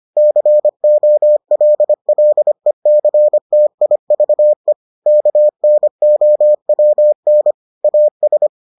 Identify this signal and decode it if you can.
Morse code, heard as a keyed tone, about 25 wpm, 600 Hz. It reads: COLLECTIVE KNOWN AS